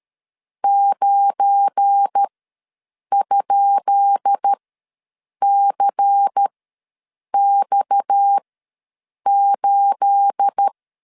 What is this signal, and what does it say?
Morse code, heard as a keyed tone, about 13 wpm, spaced as for 7 wpm, 800 Hz. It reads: 9?CX8
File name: LEZIONI / 10 SIGNS / 10C DOT ws 12wpm 3548